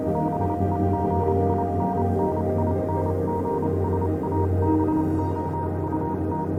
Morse code, heard as a keyed tone, about 28 wpm, spaced as for 25 wpm, 940 Hz, with noise. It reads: BEGINNINGS ABOUT
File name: Vylet Pony / aura / do you remember the song she sang (morse code version)